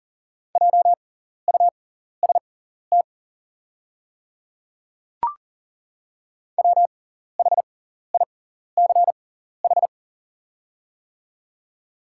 Morse code, heard as a keyed tone, about 40 wpm, spaced as for 14 wpm, 700 Hz. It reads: JUST WHICH
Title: 6x Character Spacing - Top 200 Words - 40wpm